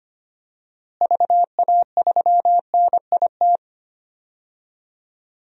Morse code, heard as a keyed tone, about 25 wpm, 700 Hz. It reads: VA3NIT
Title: DXCC-100 Call Signs - 6 Characters - Repeated 2x - 25wpm